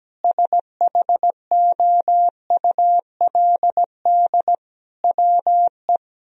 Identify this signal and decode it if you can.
Morse code, heard as a keyed tone, about 17 wpm, 700 Hz. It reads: SHOULD WE
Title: Sentences from Top 1000 Words - Review Part 3 - 17wpm